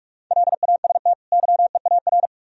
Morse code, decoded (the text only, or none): LAST YEAR